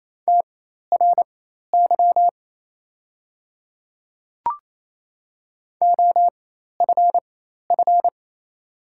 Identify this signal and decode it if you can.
Morse code, heard as a keyed tone, about 28 wpm, spaced as for 13 wpm, 700 Hz. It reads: TRY OFF
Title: 4x Character Spacing - Top 500 Words - 28wpm